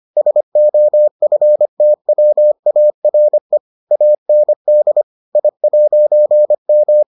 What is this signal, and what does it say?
Morse code, heard as a keyed tone, about 25 wpm, 600 Hz. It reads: SOFTWARE AND I'M